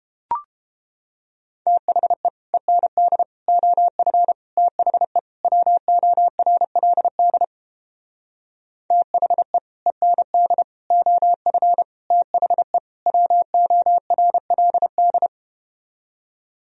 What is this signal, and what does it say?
Morse code, heard as a keyed tone, about 31 wpm, 700 Hz. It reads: THE END OF THE WORLD THE END OF THE WORLD